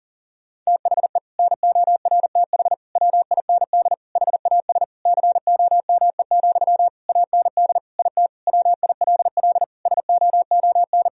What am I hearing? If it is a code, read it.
Morse code, heard as a keyed tone, about 40 wpm, 700 Hz. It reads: THE NORTH WIND HAS COME, AND IT WILL SOON
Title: Sentences from Top 1000 Words - Review Part 3 - 40wpm